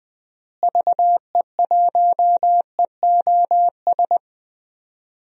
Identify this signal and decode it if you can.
Morse code, heard as a keyed tone, about 20 wpm, 700 Hz. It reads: VE1EOS
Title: DXCC-100 Call Signs - 6 Characters - Repeated 3x - 20wpm